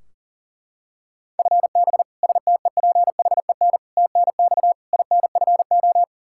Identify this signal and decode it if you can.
Morse code, heard as a keyed tone, about 40 wpm, 700 Hz. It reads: FB STEPHEN TNX INFO
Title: Practice QSO - Exchange 3 - More Info - 40wpm